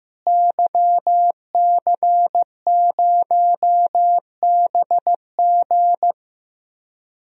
Morse code, heard as a keyed tone, about 15 wpm, 700 Hz. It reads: YC0BG
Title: DXCC-100 Call Signs - 5 Characters - Repeated 2x - 15wpm